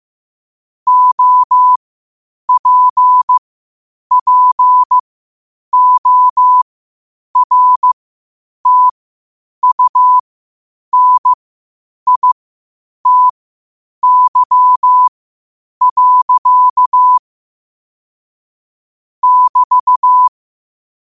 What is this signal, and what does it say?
Morse code, heard as a keyed tone, about 15 wpm, spaced as for 8 wpm, 1000 Hz. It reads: OPPORTUNITY. <BT>
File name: qotdfiles / 8wpm